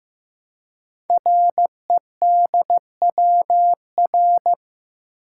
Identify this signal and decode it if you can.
Morse code, heard as a keyed tone, about 15 wpm, 700 Hz. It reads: REDWR
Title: Code Groups - 5 Letters - 15wpm